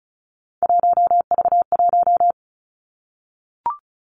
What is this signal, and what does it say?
Morse code, heard as a keyed tone, about 36 wpm, 700 Hz. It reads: 1V1
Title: Three Challenging Letter-Number Combinations 35wpm